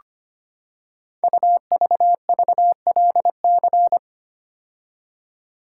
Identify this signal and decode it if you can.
Morse code, heard as a keyed tone, about 25 wpm, 700 Hz. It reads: UVVLC
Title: Code Groups - 5 Letters - 25wpm